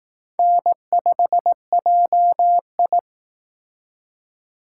Morse code, heard as a keyed tone, about 19 wpm, 700 Hz. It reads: N5JI
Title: Speed Racing - USA Call Signs 15wpm